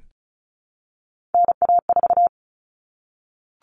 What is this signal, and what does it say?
Morse code, heard as a keyed tone, about 34 wpm, 700 Hz. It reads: NA4
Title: All Cut Numbers - 3-Digit 35wpm